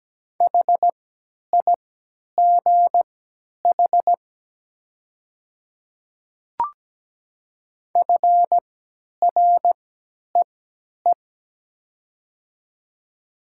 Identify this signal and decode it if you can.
Morse code, heard as a keyed tone, about 17 wpm, spaced as for 10 wpm, 700 Hz. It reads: HIGH FREE
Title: 3x Character Spacing - Top 500 Words - 17wpm